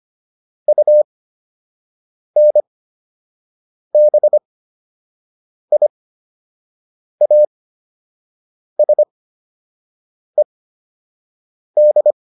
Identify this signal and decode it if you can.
Morse code, heard as a keyed tone, about 25 wpm, spaced as for 6 wpm, 600 Hz. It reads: UNBIASED